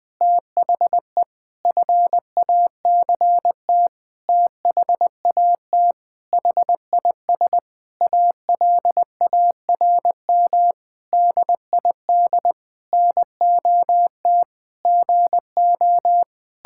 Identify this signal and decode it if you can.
Morse code, heard as a keyed tone, about 20 wpm, 700 Hz. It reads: THE FACT THAT HIS ALARM DID NOT GO
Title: Sentences with 4-Word Phrases - 20wpm